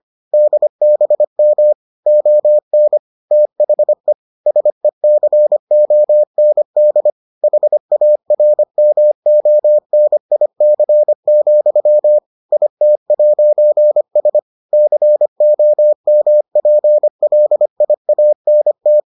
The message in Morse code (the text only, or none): DBM ON THE SECOND HARMONIC, IT'S COMPLIANT